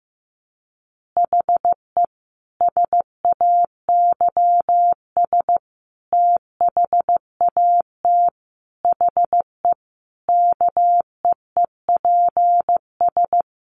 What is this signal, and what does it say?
Morse code, heard as a keyed tone, about 15 wpm, 700 Hz. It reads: HE SAYS THAT HE KEEPS
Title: Additional Easy Phrases 15wpm